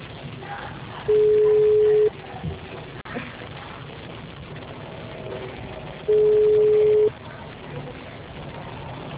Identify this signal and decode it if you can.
Ringback tone (Europe, 425 Hz).